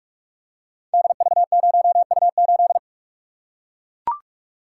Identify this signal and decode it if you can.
Morse code, heard as a keyed tone, about 44 wpm, 700 Hz. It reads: DV0U8
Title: Code Groups - 5 Letter:Numbers - 45wpm